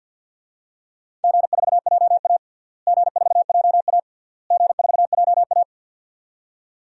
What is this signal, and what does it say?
Morse code, heard as a keyed tone, about 50 wpm, 700 Hz. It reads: G4JA G4JA G4JA
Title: DXCC-100 Call Signs - 4 Characters - Repeated 3x - 50wpm